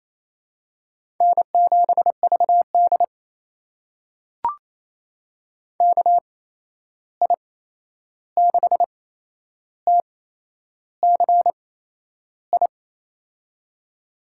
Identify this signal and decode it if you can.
Morse code, heard as a keyed tone, about 28 wpm, spaced as for 11 wpm, 700 Hz. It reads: N7VD KI6TCI
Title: USA Call Signs - 8x Character Spacing - 28wpm